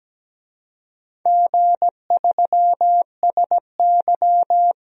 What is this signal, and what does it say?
Morse code, heard as a keyed tone, about 17 wpm, 700 Hz. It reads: G3SY
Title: DXCC-100 Call Signs - 4 Characters - Repeated 3x - 17wpm